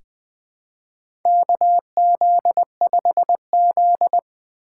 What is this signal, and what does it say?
Morse code, heard as a keyed tone, about 20 wpm, 700 Hz. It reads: KZ5Z
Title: K1USN SST QSO Practice - 20wpm